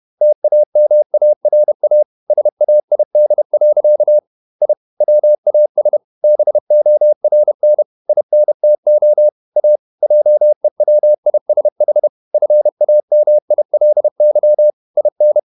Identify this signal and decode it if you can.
Morse code, heard as a keyed tone, about 31 wpm, 600 Hz. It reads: TAMARA SAID. I WAS BORN INTO A JEWISH FAMILY IN
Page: Never 31wpm | CW med Gnister